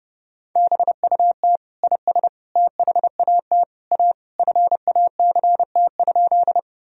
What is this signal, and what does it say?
Morse code, heard as a keyed tone, about 30 wpm, 700 Hz. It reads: BUT IS THAT A FACT?
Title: Sentences from Top 1000 Words - Review Part 3 - 30wpm